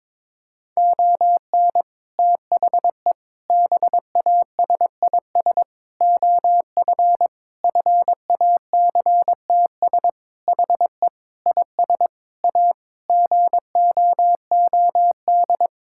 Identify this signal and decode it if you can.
Morse code, heard as a keyed tone, about 22 wpm, 700 Hz. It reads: ON THE BASIS OF FACTS HE IS A GOOD
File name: practice / 22wpm / Sentences with 3-Word Phrases - 22wpm